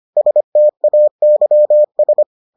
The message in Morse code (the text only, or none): STAYS